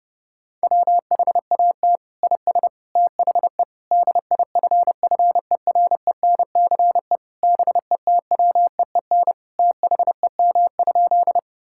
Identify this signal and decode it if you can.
Morse code, heard as a keyed tone, about 30 wpm, 700 Hz. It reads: WHAT IS THE DIFFERENCE BETWEEN THEM?
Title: Sentences from Top 1000 Words - Review Part 5 - 30wpm